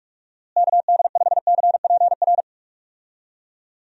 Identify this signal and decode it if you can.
Morse code, heard as a keyed tone, about 46 wpm, 700 Hz. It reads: KD5CPR
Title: USA Call Signs - 6 Characters - Repeated 2x - 45wpm